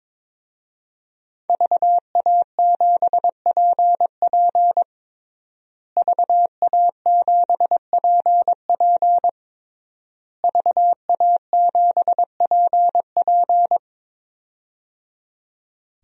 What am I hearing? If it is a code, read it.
Morse code, heard as a keyed tone, about 22 wpm, 700 Hz. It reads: VA7PP VA7PP VA7PP
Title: DXCC-100 Call Signs - 5 Characters - Repeated 3x - 22wpm